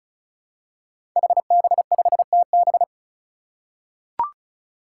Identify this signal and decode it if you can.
Morse code, heard as a keyed tone, about 36 wpm, 700 Hz. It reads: HB5TB
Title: Five Challenging Letter-Number Combinations 35wpm